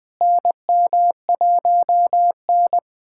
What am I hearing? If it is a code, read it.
Morse code, heard as a keyed tone, about 20 wpm, 700 Hz. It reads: NM1N
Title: USA Callsign DE Callsign - 4 Characters - 20wpm